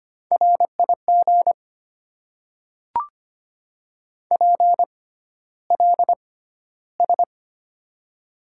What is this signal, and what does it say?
Morse code, heard as a keyed tone, about 25 wpm, spaced as for 13 wpm, 700 Hz. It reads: RIG PLS
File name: Top 90 QSO Elements - 6x Character Spacing - 25wpm